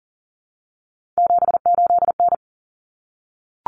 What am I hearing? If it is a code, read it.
Morse code, heard as a keyed tone, about 40 wpm, 700 Hz. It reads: 78N
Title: Common Cut Numbers - 3-Digit 40wpm